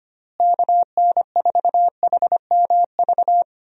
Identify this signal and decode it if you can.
Morse code, heard as a keyed tone, about 25 wpm, 700 Hz. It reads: KN4HMV
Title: USA Call Signs - 6 Characters - Repeated 3x - 25wpm